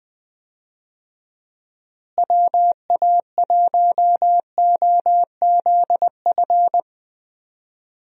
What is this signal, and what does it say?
Morse code, heard as a keyed tone, about 20 wpm, 700 Hz. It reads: WA1OZF
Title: USA Call Signs - 6 Characters - Repeated 3x - 20wpm